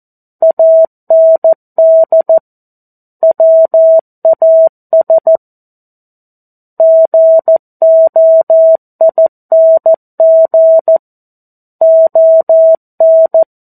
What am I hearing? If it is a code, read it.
Morse code, heard as a keyed tone, about 14 wpm, 650 Hz. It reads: AND WAS GOING ON
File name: 02-Chapter-02 14 WPM